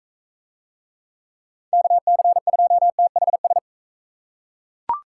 Morse code, heard as a keyed tone, about 39 wpm, 700 Hz. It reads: KC2THS